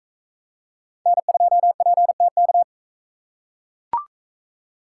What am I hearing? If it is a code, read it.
Morse code, heard as a keyed tone, about 41 wpm, 700 Hz. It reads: N2PTK